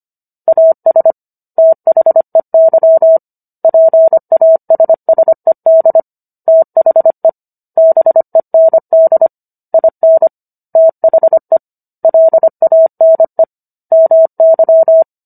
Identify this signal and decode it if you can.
Morse code, heard as a keyed tone, about 25 wpm, 650 Hz. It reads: AS THEY PASSED THE BEND IN THE LANE MY